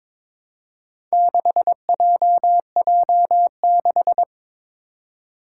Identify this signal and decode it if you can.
Morse code, heard as a keyed tone, about 21 wpm, 700 Hz. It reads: 6JJ6
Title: Four Challenging Letter-Number Combinations 22wpm